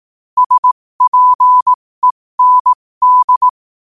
Morse code, heard as a keyed tone, about 18 wpm, spaced as for 15 wpm, 1000 Hz. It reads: SPEND
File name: qotdfiles / 15wpm